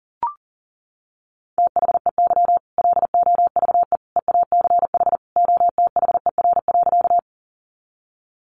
Morse code, heard as a keyed tone, about 40 wpm, 700 Hz. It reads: THEY LOVE EACH OTHER.
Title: Easy Phrases 40wpm